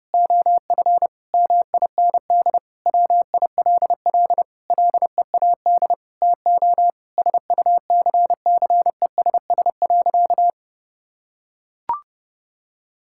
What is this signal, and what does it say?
Morse code, heard as a keyed tone, about 32 wpm, 700 Hz. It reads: OF MIND WILL LEAD TO SUCCESS.
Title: Sentences from Top 1000 Words - Review Part 6 - 30wpm